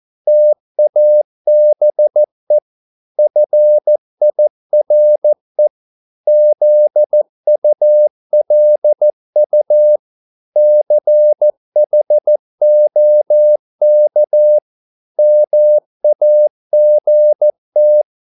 Morse code, heard as a keyed tone, about 14 wpm, 600 Hz. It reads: TABE FIRE ZULU CHOK MAGT
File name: Korte ord DK 14wpm